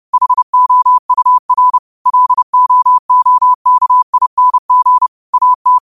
Morse code, heard as a keyed tone, about 30 wpm, 1000 Hz. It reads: HOUR LOOKING AT
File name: qotdfiles / 30wpm